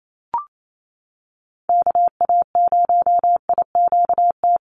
Morse code, heard as a keyed tone, about 28 wpm, 700 Hz. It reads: KA0IQT
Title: Call Signs - 28wpm